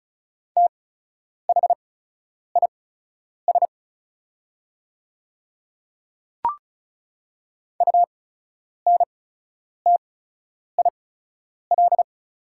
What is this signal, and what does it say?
Morse code, heard as a keyed tone, about 36 wpm, spaced as for 10 wpm, 690 Hz. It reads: THIS UNTIL
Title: 8x Character Spacing - Top 500 Words - 35wpm